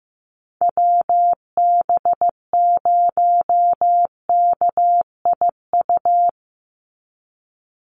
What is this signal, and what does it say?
Morse code, heard as a keyed tone, about 15 wpm, 700 Hz. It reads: WB0KIU
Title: NA SOTA Chaser Callsigns 15wpm